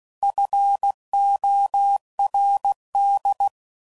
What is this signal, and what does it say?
Morse code, heard as a keyed tone, about 16 wpm, 800 Hz. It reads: FORD
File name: CW Code Groups @15WPM / fromSM's / Cities